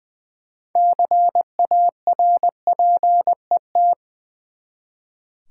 Morse code, heard as a keyed tone, about 20 wpm, 700 Hz. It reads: CARPET
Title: Six Letter Words - 20wpm